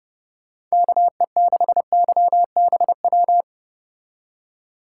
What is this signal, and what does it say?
Morse code, heard as a keyed tone, about 33 wpm, 700 Hz. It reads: KE6YBW